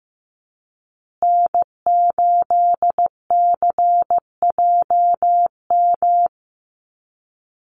Morse code, heard as a keyed tone, about 15 wpm, 700 Hz. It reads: N8CJM
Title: USA Call Signs Repeated 3x 15wpm